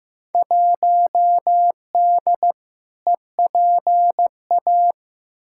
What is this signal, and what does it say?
Morse code, heard as a keyed tone, about 15 wpm, 700 Hz. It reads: 1D EPA
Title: Field Day Exchanges - 15wpm